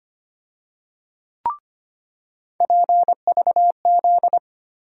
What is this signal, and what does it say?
Morse code, heard as a keyed tone, about 25 wpm, 700 Hz. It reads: PVZ
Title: USA Call Sign Postfixes - 3 Letters 25wpm